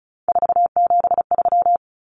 Morse code, heard as a keyed tone, about 34 wpm, 700 Hz. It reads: 473
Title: Numbers - 3-Digit 35wpm